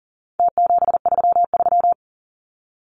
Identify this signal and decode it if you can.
Morse code, heard as a keyed tone, about 40 wpm, 700 Hz. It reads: T733
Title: Numbers - Contest Style - 4-Digit 40wpm